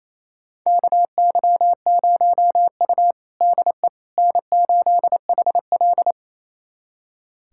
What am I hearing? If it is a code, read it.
Morse code, heard as a keyed tone, about 28 wpm, 700 Hz. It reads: KY0U DE N8HL
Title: USA Callsign DE Callsign - 4 Characters - 28wpm